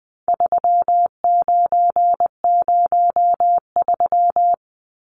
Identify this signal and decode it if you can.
Morse code, heard as a keyed tone, about 20 wpm, 700 Hz. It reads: 3903
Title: Numbers - 4-Digit 20wpm